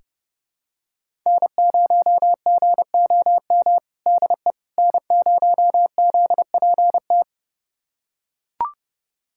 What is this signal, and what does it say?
Morse code, heard as a keyed tone, about 30 wpm, 700 Hz. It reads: N0GOM DE N0ZPT
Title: USA Callsign DE Callsign - 5 Characters - 30wpm